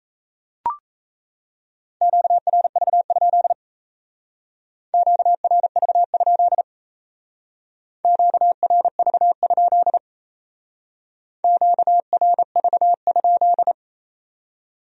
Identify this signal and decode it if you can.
Morse code, heard as a keyed tone, about 36 wpm, 700 Hz. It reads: QRV? QRV? QRV? QRV?